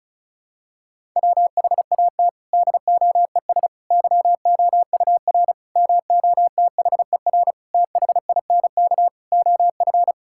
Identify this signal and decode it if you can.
Morse code, heard as a keyed tone, about 34 wpm, 700 Hz. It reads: WHAT DOES YOUR MOTHER THINK OF
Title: Sentences from Top 1000 Words - Review Part 5 - 35wpm